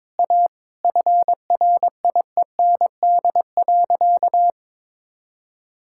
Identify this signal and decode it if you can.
Morse code, heard as a keyed tone, about 22 wpm, 700 Hz. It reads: A FRIEND.
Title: Sentences from Top 1000 Words - Review Part 6 - 22wpm